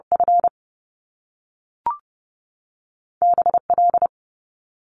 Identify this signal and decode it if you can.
Morse code, heard as a keyed tone, about 30 wpm, 700 Hz. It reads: F BL
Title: Top Two Letter Combinations in Words 30wpm